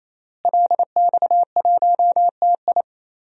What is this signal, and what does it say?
Morse code, heard as a keyed tone, about 28 wpm, 700 Hz. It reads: LX1TI
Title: DXCC-100 Call Signs - 5 Characters - Repeated 3x - 28wpm